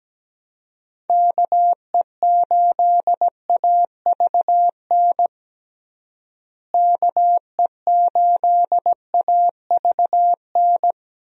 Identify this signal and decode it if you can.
Morse code, heard as a keyed tone, about 17 wpm, 700 Hz. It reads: KE8AVN KE8AVN
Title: USA Call Signs - 6 Characters - Repeated 3x - 17wpm